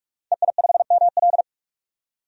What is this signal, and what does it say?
Morse code, heard as a keyed tone, about 47 wpm, 700 Hz. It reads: EI5ML
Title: DXCC-100 Call Signs - 5 Characters - 45wpm